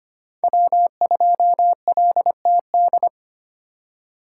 Morse code, heard as a keyed tone, about 25 wpm, 700 Hz. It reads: W2LTD
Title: USA Call Signs - 5 Characters - 2x Character Spacing - 25wpm